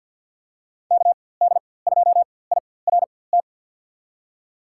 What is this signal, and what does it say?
Morse code, heard as a keyed tone, about 53 wpm, spaced as for 24 wpm, 700 Hz. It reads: KD2IRT
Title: USA Call Signs - 6 Characters - 4x Character Spacing - 50wpm